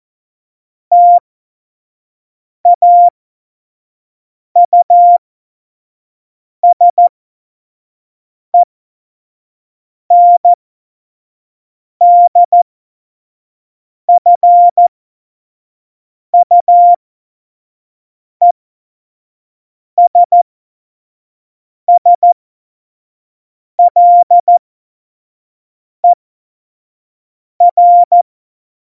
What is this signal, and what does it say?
Morse code, heard as a keyed tone, about 14 wpm, spaced as for 5 wpm, 700 Hz. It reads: TAUSENDFUESSLER
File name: Morsekurs / Witze / Witz 22. Mai 2018, 25 F70